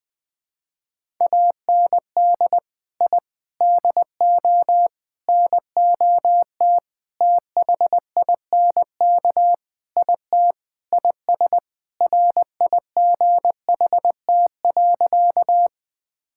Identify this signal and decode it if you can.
Morse code, heard as a keyed tone, about 20 wpm, 700 Hz. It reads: AND I DO NOT THINK IT IS RIGHT.
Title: Sentences with 5-Word Phrases - 20wpm